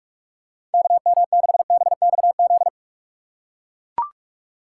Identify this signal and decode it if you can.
Morse code, heard as a keyed tone, about 44 wpm, 700 Hz. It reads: KM6BXZ
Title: USA Call Signs - 6 Characters - 4x Character Spacing - 45wpm